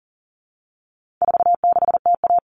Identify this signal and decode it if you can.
Morse code, heard as a keyed tone, about 40 wpm, 700 Hz. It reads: 46TA
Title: All Cut Numbers - 4-Digit 40wpm